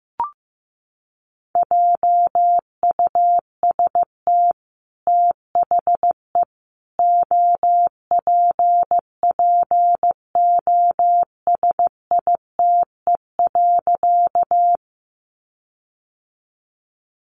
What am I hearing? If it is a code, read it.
Morse code, heard as a keyed tone, about 15 wpm, 700 Hz. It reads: JUST THE OPPOSITE.
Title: Sentences from Top 1300 Words 15wpm